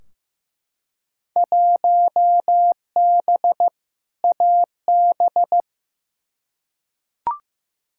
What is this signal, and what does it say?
Morse code, heard as a keyed tone, about 15 wpm, 700 Hz. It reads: 1B AB
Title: Field Day Exchanges - 15wpm